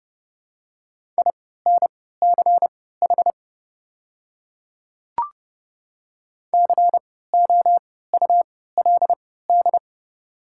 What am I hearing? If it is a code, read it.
Morse code, heard as a keyed tone, about 30 wpm, spaced as for 17 wpm, 700 Hz. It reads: INCH COULD
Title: 3x Character Spacing - Top 500 Words - 30wpm